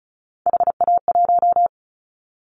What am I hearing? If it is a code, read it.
Morse code, heard as a keyed tone, about 36 wpm, 700 Hz. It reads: HA1
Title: Callsign Prefixes for Top 50 DXCC - 35wpm